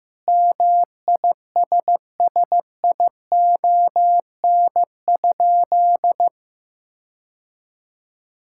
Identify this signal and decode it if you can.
Morse code, heard as a keyed tone, about 15 wpm, 700 Hz. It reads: MISSION?
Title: Sentences from Top 1600 Words 15wpm